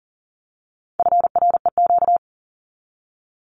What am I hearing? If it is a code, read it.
Morse code, heard as a keyed tone, about 38 wpm, 700 Hz. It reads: FREQ
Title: Abbreviations 40wpm